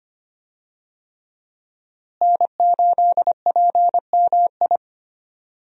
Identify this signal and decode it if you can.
Morse code, heard as a keyed tone, about 25 wpm, 700 Hz. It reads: N8PMI
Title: USA Call Signs - 5 Characters - Repeated 3x - 25wpm